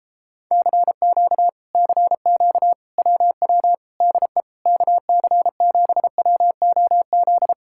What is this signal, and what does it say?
Morse code, heard as a keyed tone, about 37 wpm, 700 Hz. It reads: CQ CQ WW DE KC7WOZ